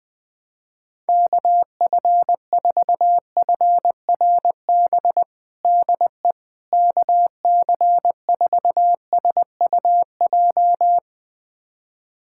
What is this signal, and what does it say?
Morse code, heard as a keyed tone, about 21 wpm, 700 Hz. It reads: KF4FRB DE KC4SUJ
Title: USA Callsign DE Callsign - 6 Characters - 20wpm